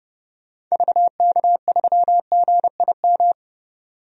Morse code, heard as a keyed tone, about 32 wpm, 700 Hz. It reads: VK3GIM
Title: DXCC-100 Call Signs - 6 Characters - 30wpm